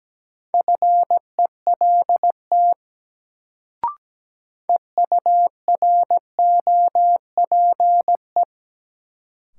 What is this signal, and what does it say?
Morse code, heard as a keyed tone, about 17 wpm, 700 Hz. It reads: FELT EUROPE
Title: Sets of 5 Words Encore - New Word Drill - 17wpm